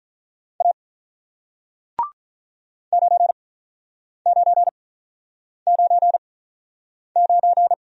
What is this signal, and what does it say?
Morse code, heard as a keyed tone, about 48 wpm, spaced as for 9 wpm, 700 Hz. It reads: A 9999